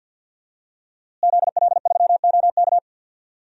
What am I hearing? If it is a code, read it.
Morse code, heard as a keyed tone, about 50 wpm, 700 Hz. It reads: ZL3OK
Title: DXCC-100 Call Signs - 5 Characters - Repeated 2x - 50wpm